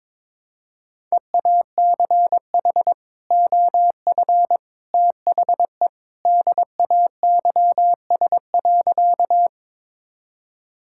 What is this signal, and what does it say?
Morse code, heard as a keyed tone, about 22 wpm, 700 Hz. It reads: EACH OF THE DAYS.
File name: practice / 22wpm / Sentences with 3-Word Phrases - 22wpm